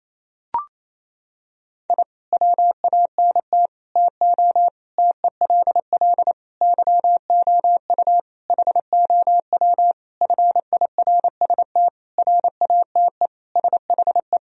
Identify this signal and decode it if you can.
Morse code, heard as a keyed tone, about 28 wpm, 700 Hz. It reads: I WANT TO TELL YOU HOW FIRST RATE SHE
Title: Sentences from Top 1000 Words - Review Part 4 - 28wpm